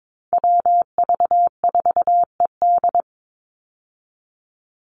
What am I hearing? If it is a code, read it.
Morse code, heard as a keyed tone, about 22 wpm, 700 Hz. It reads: WV4ED
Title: USA Call Signs 22wpm